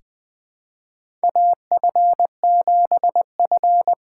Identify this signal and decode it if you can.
Morse code, heard as a keyed tone, about 20 wpm, 700 Hz. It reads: AF7F
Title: USA Call Signs - 5 Characters - 6x Character Spacing - 20wpm